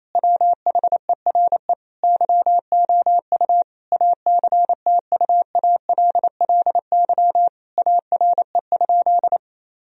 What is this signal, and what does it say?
Morse code, heard as a keyed tone, about 28 wpm, 700 Hz. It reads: WHERE YOU ACTUALLY ARE?
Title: Sentences from Top 1000 Words - Review Part 4 - 28wpm